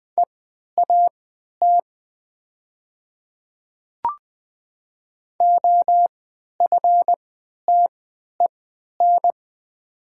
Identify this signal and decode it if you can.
Morse code, heard as a keyed tone, about 20 wpm, spaced as for 11 wpm, 700 Hz. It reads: EAT OFTEN
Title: 3x Character Spacing - Top 300 Words - 20wpm